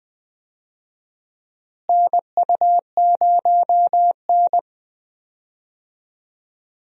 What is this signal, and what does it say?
Morse code, heard as a keyed tone, about 20 wpm, 700 Hz. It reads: NU0N
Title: USA Call Signs - 4 Characters - Repeated 2x - 20wpm